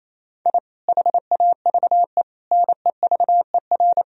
Morse code, heard as a keyed tone, about 28 wpm, 700 Hz. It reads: I HAVE NEVER
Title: Sentences from Top 1000 Words - Review Part 3 - 28wpm